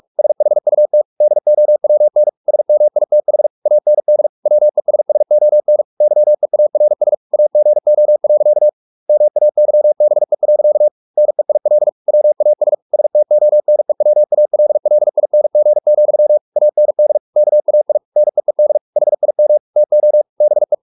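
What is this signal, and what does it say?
Morse code, heard as a keyed tone, about 44 wpm, 600 Hz. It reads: SHUT DOWN SMITH AND WESSON YEARS AGO. MAYBE. NEIL WAS STONEWALLING, AND KAI NEED HIM TO BE
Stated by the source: Morse practice at 45 wpm, the novel Never by Ken Follett in English, from CW med Gnister.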